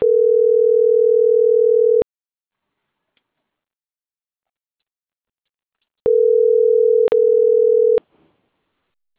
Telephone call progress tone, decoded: ringback tone

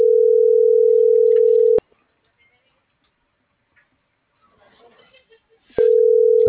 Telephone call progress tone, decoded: ringback tone

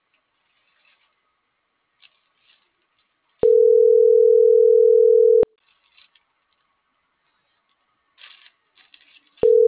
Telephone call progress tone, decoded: ringback tone

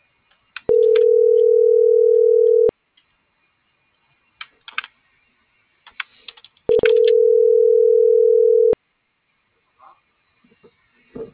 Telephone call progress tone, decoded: ringback tone

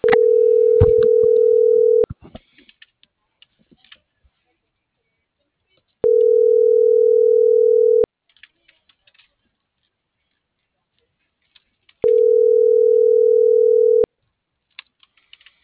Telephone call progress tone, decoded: ringback tone

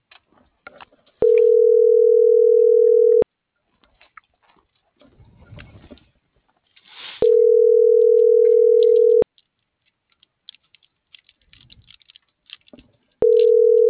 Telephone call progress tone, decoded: ringback tone

